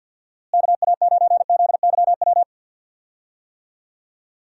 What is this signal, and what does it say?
Morse code, heard as a keyed tone, about 52 wpm, 700 Hz. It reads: KA9ZYW
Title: USA Call Signs - 6 Characters - Repeated 3x - 50wpm